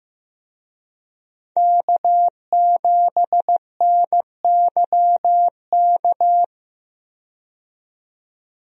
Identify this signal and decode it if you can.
Morse code, heard as a keyed tone, about 15 wpm, 700 Hz. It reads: K7NYK
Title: USA Call Signs - 5 Characters - 15wpm